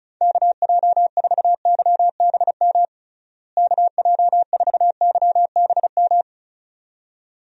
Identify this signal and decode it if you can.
Morse code, heard as a keyed tone, about 36 wpm, 700 Hz. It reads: KJ4YBM KJ4YBM
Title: USA Call Signs - 6 Characters - Repeated 2x - 35wpm